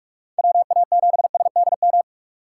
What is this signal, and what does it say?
Morse code, heard as a keyed tone, about 44 wpm, 700 Hz. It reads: WA7SDM
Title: USA Call Signs - 6 Characters - 6x Character Spacing - 45wpm